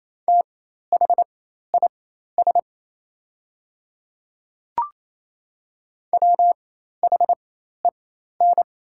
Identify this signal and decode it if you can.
Morse code, heard as a keyed tone, about 28 wpm, spaced as for 13 wpm, 700 Hz. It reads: THIS WHEN
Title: 4x Character Spacing - Top 200 Words - 28wpm